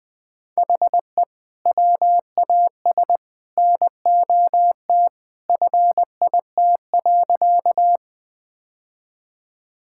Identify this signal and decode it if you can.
Morse code, heard as a keyed tone, about 20 wpm, 700 Hz. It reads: HE WAS NOT FIT.
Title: Sentences from Top 1000 Words - Review Part 5 - 20wpm